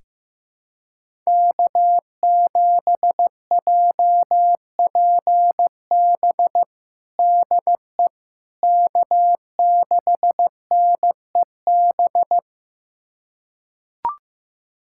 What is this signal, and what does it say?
Morse code, heard as a keyed tone, about 15 wpm, 700 Hz. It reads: K7JPB DE K6NEB
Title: USA Callsign DE Callsign 15wpm